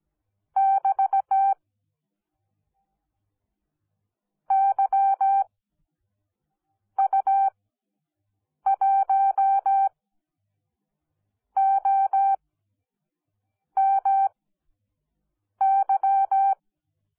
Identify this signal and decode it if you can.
Morse code, heard as a keyed tone, about 17 wpm, spaced as for 5 wpm, 780 Hz. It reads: <BT> YU1OMY